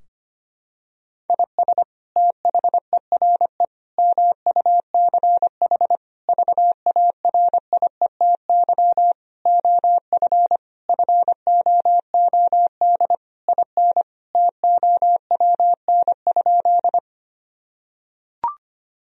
Morse code, heard as a keyed tone, about 25 wpm, 700 Hz. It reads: IS THERE MUCH VARIETY OF FOOD IN TOWN?
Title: Sentences from Top 1000 Words - Review Part 4 - 25wpm